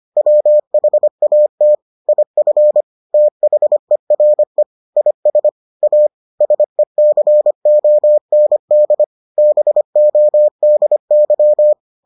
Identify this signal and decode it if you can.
Morse code, heard as a keyed tone, about 25 wpm, 600 Hz. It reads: WHAT IF THERE IS A SECOND BODY